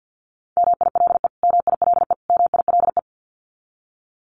Morse code, heard as a keyed tone, about 50 wpm, 700 Hz. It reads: MILE MILE MILE
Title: Top 500 Words Repeated 50wpm